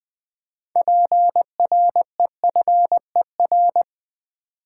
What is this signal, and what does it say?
Morse code, heard as a keyed tone, about 20 wpm, 700 Hz. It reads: PREFER
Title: Sentences from Top 1100 Words - New Word Drill v2 - 20wpm